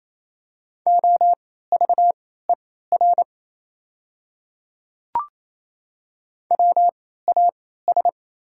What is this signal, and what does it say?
Morse code, heard as a keyed tone, about 28 wpm, spaced as for 16 wpm, 690 Hz. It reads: OVER WAS
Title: 3x Character Spacing - Top 100 Words - 28wpm